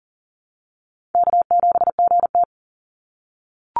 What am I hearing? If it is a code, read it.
Morse code, heard as a keyed tone, about 41 wpm, 700 Hz. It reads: K7GT
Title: NA SOTA Chaser Callsigns 40wpm